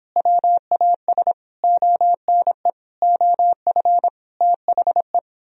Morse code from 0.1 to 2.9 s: WAS ONE

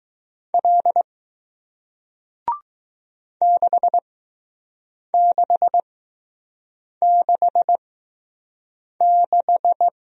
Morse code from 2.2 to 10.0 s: 6666

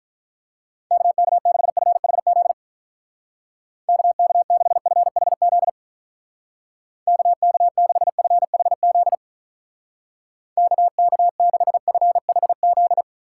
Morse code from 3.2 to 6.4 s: KK6FHZ

